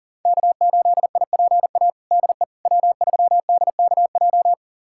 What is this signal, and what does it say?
Morse code, heard as a keyed tone, about 40 wpm, 700 Hz. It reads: K8IPA DE W3DKJ
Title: USA Callsign DE Callsign - 5 Characters - 40wpm